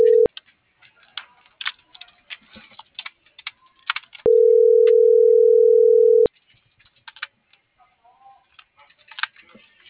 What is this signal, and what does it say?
Ringback tone (North America), 440 + 480 Hz.